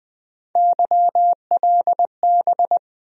Morse code, heard as a keyed tone, about 20 wpm, 700 Hz. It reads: YLB
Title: USA Call Sign Postfixes - 3 Letters 20wpm